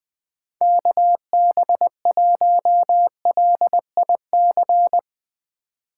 Morse code, heard as a keyed tone, about 22 wpm, 700 Hz. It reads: KB1LIC
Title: Speed Racing - USA 2x3 Call Signs 20wpm